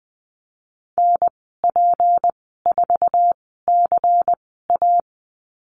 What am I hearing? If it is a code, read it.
Morse code, heard as a keyed tone, about 20 wpm, spaced as for 15 wpm, 700 Hz. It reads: NP4CA
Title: 193 Lesson 31g - Letter v - Call Signs 20wpm